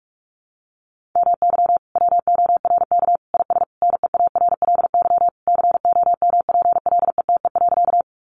Morse code, heard as a keyed tone, about 44 wpm, 700 Hz. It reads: MY WORK IS NEARLY COMPLETE.
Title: Sentences from Top 1300 Words 45wpm